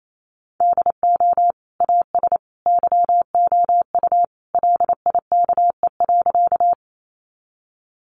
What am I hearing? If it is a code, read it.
Morse code, heard as a keyed tone, about 28 wpm, 700 Hz. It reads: DO AS YOU LIKE.
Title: Sentences from Top 100 Words 28wpm